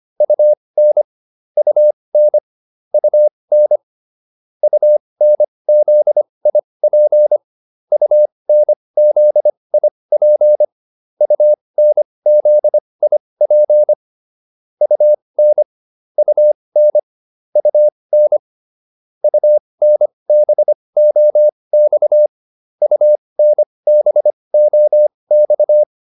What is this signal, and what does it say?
Morse code, heard as a keyed tone, about 25 wpm, spaced as for 20 wpm, 600 Hz. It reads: UN UN UN UNZIP UNZIP UNZIP UN UN UN UNBOX UNBOX